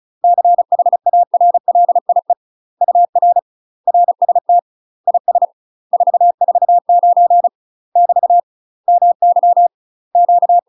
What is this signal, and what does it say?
Morse code, heard as a keyed tone, about 33 wpm, 700 Hz. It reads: CHARLIE UR RST IS 449 <BT> MY Q
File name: UK MASTER QSOs TRAINING / 35WPM / UK MASTER QSOs 0014